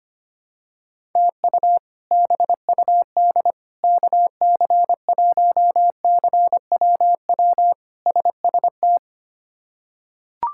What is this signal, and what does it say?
Morse code, heard as a keyed tone, about 25 wpm, 700 Hz. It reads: TU BUD KC1CWW SST
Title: K1USN SST QSO Practice - 25wpm